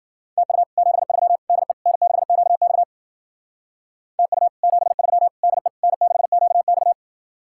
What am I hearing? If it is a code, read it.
Morse code, heard as a keyed tone, about 57 wpm, 700 Hz. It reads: TU 73 DE N6QX TU 73 DE N6QX